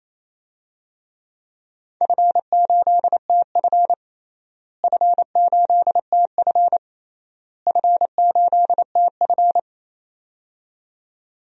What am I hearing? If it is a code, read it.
Morse code, heard as a keyed tone, about 28 wpm, 700 Hz. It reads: F8TF F8TF F8TF
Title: DXCC-100 Call Signs - 4 Characters - Repeated 3x - 28wpm